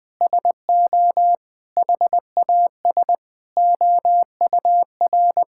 Morse code, heard as a keyed tone, about 20 wpm, 700 Hz. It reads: SO HAS OUR
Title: Sentences with 9-Word Phrases - Spoken Beginning - 20wpm